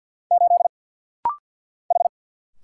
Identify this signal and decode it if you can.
Morse code, heard as a keyed tone, about 55 wpm, 700 Hz. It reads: 8 H
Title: Single Letter-Number Alt - Rapid-Fire 50wpm